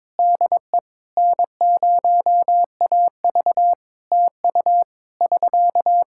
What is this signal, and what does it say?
Morse code, heard as a keyed tone, about 21 wpm, 700 Hz. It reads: DE N0AV TU <SK>
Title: K1USN SST QSO Practice 22wpm